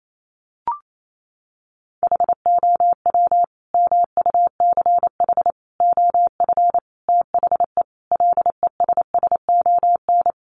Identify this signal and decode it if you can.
Morse code, heard as a keyed tone, about 28 wpm, 700 Hz. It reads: HOW MUCH OF THE LESSON
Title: Sentences from Top 1200 Words 28wpm